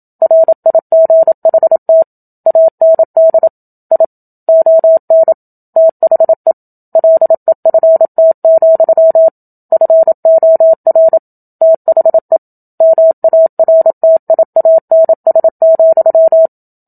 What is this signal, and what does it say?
Morse code, heard as a keyed tone, about 27 wpm, 650 Hz. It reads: RIGHT AND I ON THE LEFT, FOR THE MARTIANS,
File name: Book Two / 01-Chapter 1 - 27 WPM